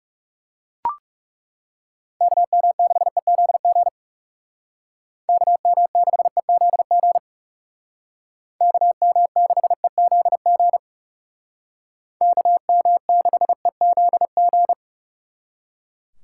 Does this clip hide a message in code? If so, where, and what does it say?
Yes: Morse code from 2.2 to 14.7 s, heard as a keyed tone, about 39 wpm, 700 Hz. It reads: KM6EZG KM6EZG KM6EZG KM6EZG